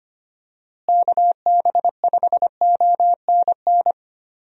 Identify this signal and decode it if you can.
Morse code, heard as a keyed tone, about 25 wpm, 700 Hz. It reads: KB5ONN
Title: USA Call Signs - 6 Characters - Repeated 3x - 25wpm